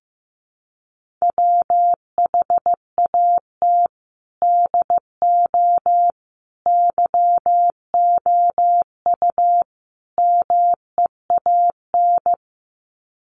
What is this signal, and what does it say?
Morse code, heard as a keyed tone, about 15 wpm, 700 Hz. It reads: WHAT DO YOU MEAN
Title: Sets of 4 Words 15wpm